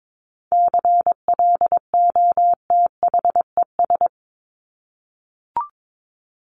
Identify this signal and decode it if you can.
Morse code, heard as a keyed tone, about 22 wpm, 700 Hz. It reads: CLOTHES
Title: Top 1751-2000 Words 22wpm